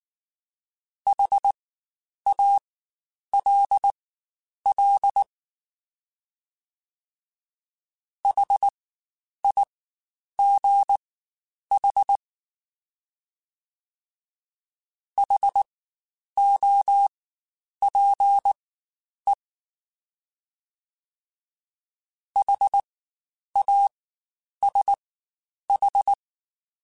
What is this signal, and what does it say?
Morse code, heard as a keyed tone, about 19 wpm, spaced as for 9 wpm, 800 Hz. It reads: HALL HIGH HOPE HASH
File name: Spaced Alphas / Spaced Alphas mp3s / Hs sp @18